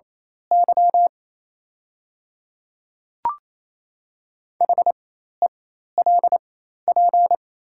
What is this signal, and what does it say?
Morse code, heard as a keyed tone, about 28 wpm, spaced as for 13 wpm, 700 Hz. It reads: Y HELP
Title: 4x Character Spacing - Top 200 Words - 28wpm